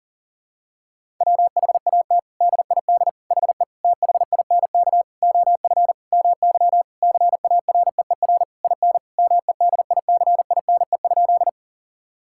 Morse code, heard as a keyed tone, about 40 wpm, 700 Hz. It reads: WHAT DID HE THINK OF MY CAREER IN MEDICINE?
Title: Sentences from Top 1000 Words - Review Part 6 - 40wpm